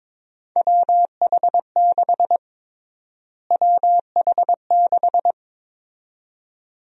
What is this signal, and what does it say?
Morse code, heard as a keyed tone, about 22 wpm, 700 Hz. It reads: WH6 WH6
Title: USA Call Sign Prefixes - Repeated 2x - 22wpm